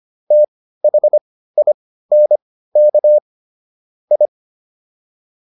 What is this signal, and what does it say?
Morse code, heard as a keyed tone, about 25 wpm, spaced as for 15 wpm, 600 Hz. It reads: THINK I